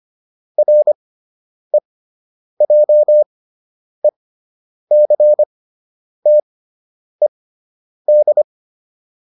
Morse code, heard as a keyed tone, about 25 wpm, spaced as for 9 wpm, 600 Hz. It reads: REJECTED